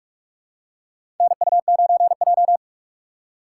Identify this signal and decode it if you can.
Morse code, heard as a keyed tone, about 46 wpm, 700 Hz. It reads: NU9J
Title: USA Call Signs - 4 Characters - 8x Character Spacing - 45wpm